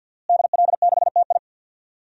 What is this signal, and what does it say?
Morse code, heard as a keyed tone, about 52 wpm, 700 Hz. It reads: DL6TI
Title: DXCC-100 Call Signs - 5 Characters - Repeated 2x - 50wpm